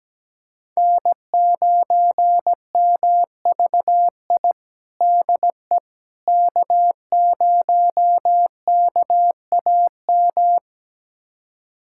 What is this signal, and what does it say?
Morse code, heard as a keyed tone, about 17 wpm, 700 Hz. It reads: N9MVI DE K0KAM
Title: USA Callsign DE Callsign - 5 Characters - 17wpm